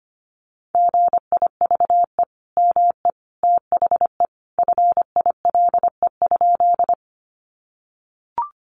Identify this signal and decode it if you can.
Morse code, heard as a keyed tone, about 25 wpm, 700 Hz. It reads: GIVE ME THE FILE?
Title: Sentences from Top 1200 Words 25wpm